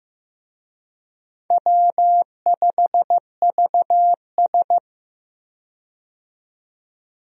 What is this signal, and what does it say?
Morse code, heard as a keyed tone, about 15 wpm, 700 Hz. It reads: W5VS
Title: USA Call Signs - 4 Characters - 15wpm